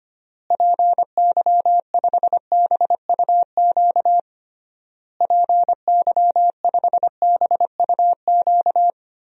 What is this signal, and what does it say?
Morse code, heard as a keyed tone, about 25 wpm, 700 Hz. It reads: PY5BUQ PY5BUQ
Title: DXCC-100 Call Signs - 6 Characters - Repeated 3x - 25wpm